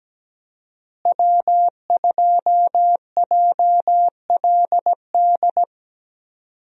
Morse code, heard as a keyed tone, about 17 wpm, 700 Hz. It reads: W2JLD
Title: USA Call Signs - 5 Characters - Repeated 2x - 17wpm